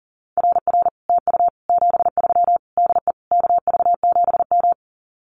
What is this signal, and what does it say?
Morse code, heard as a keyed tone, about 40 wpm, 700 Hz. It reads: RR TU 73 DE KV7M
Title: Common QSO Phrases Encore 40wpm